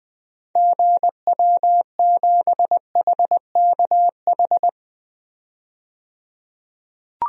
Morse code, heard as a keyed tone, about 20 wpm, 700 Hz. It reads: GW7HKH
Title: DXCC-100 Call Signs - 6 Characters - 20wpm